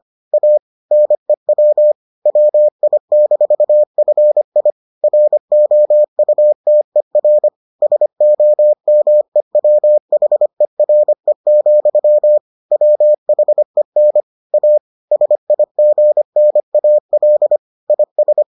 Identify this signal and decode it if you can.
Morse code, heard as a keyed tone, about 25 wpm, 600 Hz. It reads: A NEW WI-FI ROUTER SOMEWHERE, WHEN A SIGNAL IS